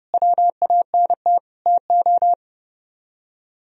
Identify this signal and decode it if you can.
Morse code, heard as a keyed tone, about 36 wpm, 700 Hz. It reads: WANT TO
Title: Speed-Racing - Sets of 5 Words 30wpm